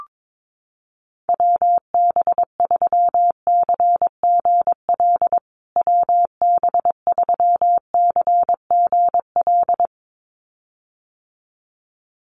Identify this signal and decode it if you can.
Morse code, heard as a keyed tone, about 22 wpm, 700 Hz. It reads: WB3CGL WB3CGL
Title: USA Call Signs Repeated 2x 22wpm